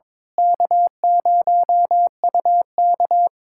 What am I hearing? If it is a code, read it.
Morse code, heard as a keyed tone, about 22 wpm, 700 Hz. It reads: K0UK
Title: K1USN SST QSO Practice - 22wpm